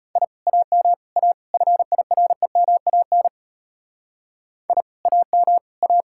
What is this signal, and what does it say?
Morse code, heard as a keyed tone, about 32 wpm, 700 Hz. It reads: I AM A FIREMAN I AM A